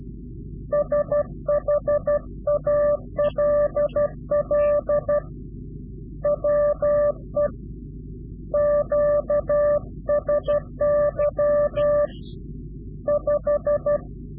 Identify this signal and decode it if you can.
Morse code, heard as a keyed tone, about 12 wpm, 590 Hz. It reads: SHALL WE QSY 5
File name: CW-QSO / Directional Call / Directional QSO